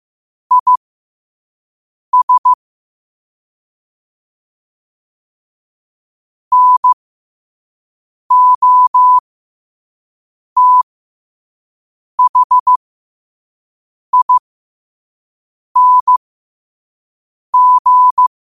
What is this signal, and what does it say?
Morse code, heard as a keyed tone, about 15 wpm, spaced as for 5 wpm, 1000 Hz. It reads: IS NOTHING
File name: qotdfiles / 5wpm